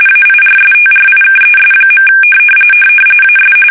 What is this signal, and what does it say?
RTTY, about 45.45 baud. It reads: AIRSTRIKES. AS
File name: fsk850 / battleshipnj